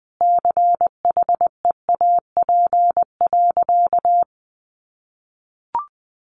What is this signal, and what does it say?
Morse code, heard as a keyed tone, about 20 wpm, 700 Hz. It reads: CHEAP.
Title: Sentences from Top 1200 Words 20wpm